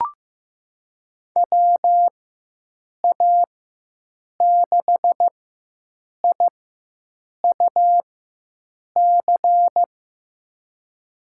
Morse code, heard as a keyed tone, about 15 wpm, spaced as for 7 wpm, 700 Hz. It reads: WA6IUC
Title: USA Call Signs - 6 Characters - 4x Character Spacing - 15wpm